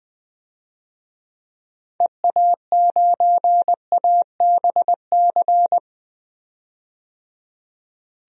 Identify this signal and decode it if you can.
Morse code, heard as a keyed tone, about 20 wpm, 700 Hz. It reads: EA9ABC
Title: DXCC-100 Call Signs - 6 Characters - 20wpm